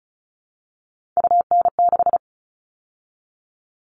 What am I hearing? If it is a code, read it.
Morse code, heard as a keyed tone, about 34 wpm, 700 Hz. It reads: UN6
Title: All Cut Numbers - 3-Digit 35wpm